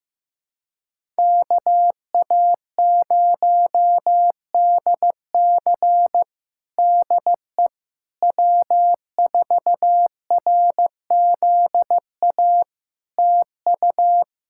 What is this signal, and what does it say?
Morse code, heard as a keyed tone, about 16 wpm, 700 Hz. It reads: KA0DC DE W4RZA TU
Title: Practice QSO - Exchange 4 - Ending - 15wpm